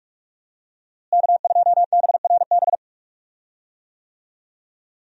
Morse code, heard as a keyed tone, about 41 wpm, 700 Hz. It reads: K2BRB
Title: USA Call Signs - 5 Characters - Repeated 2x - 45wpm